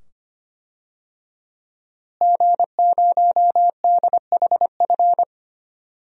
Morse code, heard as a keyed tone, about 25 wpm, 700 Hz. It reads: G0DHF
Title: DXCC-100 Call Signs - 5 Characters - Repeated 2x - 25wpm